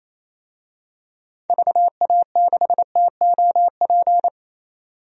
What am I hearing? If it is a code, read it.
Morse code, heard as a keyed tone, about 28 wpm, 700 Hz. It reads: VA6TOP
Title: DXCC-100 Call Signs - 6 Characters - 28wpm